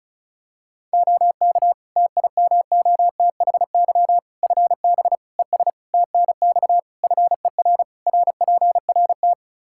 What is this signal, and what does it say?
Morse code, heard as a keyed tone, about 36 wpm, 700 Hz. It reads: OK TIMOTHY FB ES TNX FER RPRT
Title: Practice QSO - Exchange 2 - Info - 35wpm